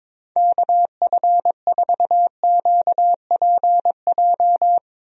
Morse code, heard as a keyed tone, about 21 wpm, 700 Hz. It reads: KF4QPJ
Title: USA Call Signs - 6 Characters - 22wpm